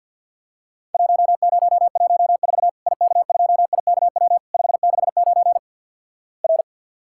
Morse code, heard as a keyed tone, about 50 wpm, 700 Hz. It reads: 1014 IK2IKW 569 R
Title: Worked All Europe DX Contest - QTC Exchange - 50wpm